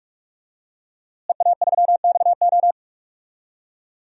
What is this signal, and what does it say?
Morse code, heard as a keyed tone, about 46 wpm, 700 Hz. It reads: EA3XO
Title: DXCC-100 Call Signs - 5 Characters - Repeated 3x - 45wpm